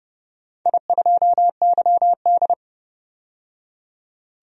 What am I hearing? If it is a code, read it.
Morse code, heard as a keyed tone, about 30 wpm, 700 Hz. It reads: I2YD
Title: DXCC-100 Call Signs - 4 Characters - 30wpm